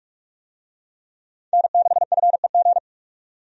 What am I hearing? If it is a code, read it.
Morse code, heard as a keyed tone, about 46 wpm, 700 Hz. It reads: N6FEG
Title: USA Call Signs - 5 Characters - 45wpm